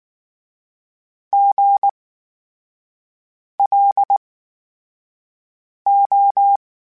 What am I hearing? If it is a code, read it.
Morse code, heard as a keyed tone, about 19 wpm, spaced as for 5 wpm, 800 Hz. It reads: GLO
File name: Special INGs / Special INGs-mp3s / VerySlowToFaster Ing @18